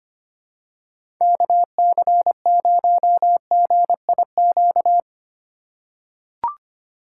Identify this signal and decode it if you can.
Morse code, heard as a keyed tone, about 25 wpm, 700 Hz. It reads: KC0GIQ
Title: USA Call Signs - 6 Characters - 3x Character Spacing - 25wpm